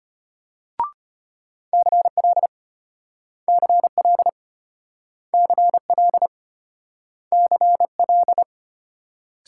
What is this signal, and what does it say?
Morse code, heard as a keyed tone, about 34 wpm, 700 Hz. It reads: CL CL CL CL